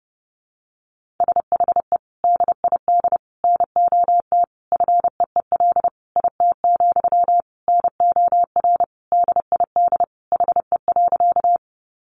Morse code, heard as a keyed tone, about 30 wpm, 700 Hz. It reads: SHE DID NOT FEEL IT, NOR DID HE.
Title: Sentences from Top 1500 Words 30wpm